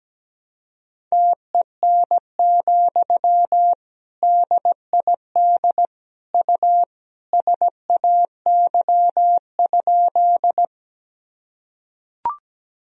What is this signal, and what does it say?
Morse code, heard as a keyed tone, about 17 wpm, 700 Hz. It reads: TEN, DID U SAY?
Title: Sentences from Top 400 Words and Abbreviations 17wpm